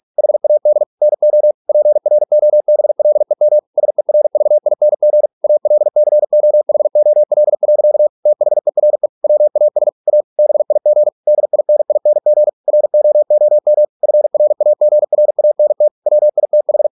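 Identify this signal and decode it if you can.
Morse code, heard as a keyed tone, about 46 wpm, 600 Hz. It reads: HAD NO PROBLEM SERVING ALCOHOL. THERE WAS A BIG DINING ROOM FRAGRANT WITH
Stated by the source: Morse practice at 45 wpm, the novel Never by Ken Follett in English, from CW med Gnister.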